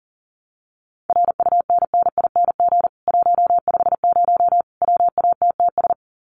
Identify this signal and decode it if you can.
Morse code, heard as a keyed tone, about 40 wpm, 700 Hz. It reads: RUNNING 150 WATTS
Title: Common QSO Phrases 40wpm